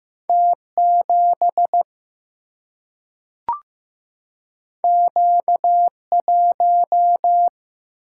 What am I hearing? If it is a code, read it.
Morse code, heard as a keyed tone, about 15 wpm, 700 Hz. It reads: T7 Q1
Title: Single-Letter Single-Number 15wpm